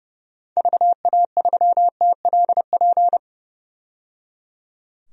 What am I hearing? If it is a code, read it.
Morse code, heard as a keyed tone, about 30 wpm, 700 Hz. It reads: VA3TLP
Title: DXCC-100 Call Signs - 6 Characters - 30wpm